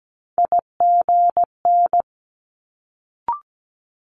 Morse code, heard as a keyed tone, about 17 wpm, 700 Hz. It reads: IGN
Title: Multi-letter Phonemes Quick Fire 17wpm